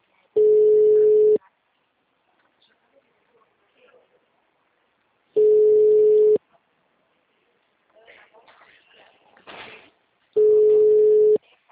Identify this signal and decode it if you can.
Ringback tone (Europe, 425 Hz), with speech.